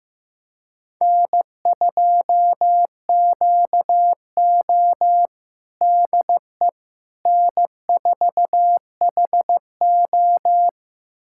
Morse code, heard as a keyed tone, about 15 wpm, 700 Hz. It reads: N2QO DE N4HO
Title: USA Callsign DE Callsign - 4 Characters - 15wpm